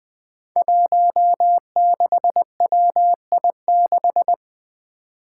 Morse code, heard as a keyed tone, about 20 wpm, 700 Hz. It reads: 16WI6
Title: Five Challenging Letter-Number Combinations 20wpm